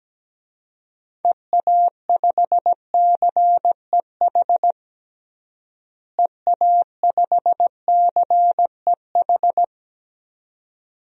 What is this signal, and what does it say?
Morse code, heard as a keyed tone, about 17 wpm, 700 Hz. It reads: EA5CEH EA5CEH
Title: DXCC-100 Call Signs - 6 Characters - Repeated 2x - 17wpm